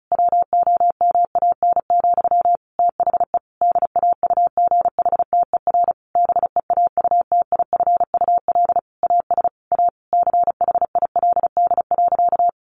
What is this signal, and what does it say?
Morse code, heard as a keyed tone, about 34 wpm, 700 Hz. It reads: WOMAN, THE DAUGHTER BEAUTIFUL AS A CHILD.
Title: Sentences from Top 1500 Words 35wpm